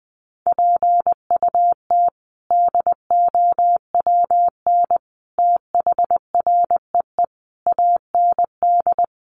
Morse code, heard as a keyed tone, about 20 wpm, 700 Hz. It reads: PUT DOWN THREE AND
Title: Sentences from Top 400 Words 20wpm